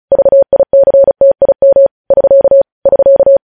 Morse code, heard as a keyed tone, about 36 wpm, 550 Hz. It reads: VICTIM <SK> <SK>
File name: podcast / News headlines-1x-35x35